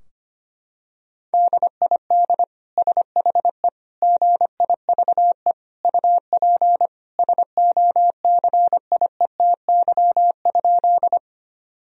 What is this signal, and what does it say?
Morse code, heard as a keyed tone, about 25 wpm, 700 Hz. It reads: DID SHE GIVE UP SOCIETY?
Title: Sentences from Top 1000 Words - Review Part 4 - 25wpm